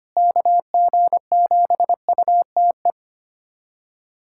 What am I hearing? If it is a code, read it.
Morse code, heard as a keyed tone, about 25 wpm, 700 Hz. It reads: KG7UTE
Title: USA Callsign DE Callsign - 6 Characters - 25wpm